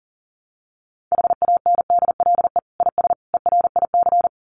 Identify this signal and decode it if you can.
Morse code, heard as a keyed tone, about 40 wpm, 700 Hz. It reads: HANDLE IS ERIC
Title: Common QSO Phrases 40wpm